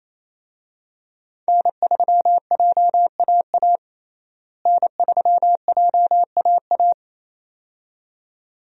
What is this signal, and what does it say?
Morse code, heard as a keyed tone, about 28 wpm, 700 Hz. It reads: N3JAA N3JAA
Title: USA Call Signs - 5 Characters - Repeated 2x - 28wpm